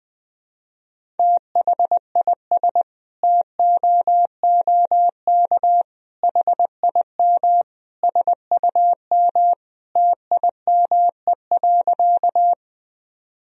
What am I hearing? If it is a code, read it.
Morse code, heard as a keyed tone, about 20 wpm, 700 Hz. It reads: THIS TOOK HIM SUM TIME.
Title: Sentences from Top 300 Words and Abbreviations 20wpm